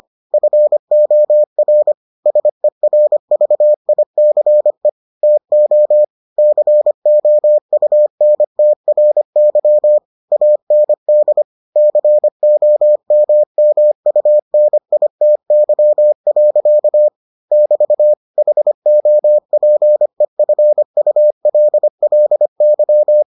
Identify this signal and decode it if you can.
Morse code, heard as a keyed tone, about 25 wpm, 600 Hz. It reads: FOR SERVICE TO COUNTRY AND COMMUNITY. <BT> HOPEFULLY